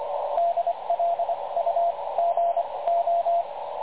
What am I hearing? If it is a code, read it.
Morse code, heard as a keyed tone, about 24 wpm, spaced as for 20 wpm, 680 Hz, with radio noise. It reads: DLUGO
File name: dlazaawansowanych / nagraniazpasma / websdr recording 3567.0kHz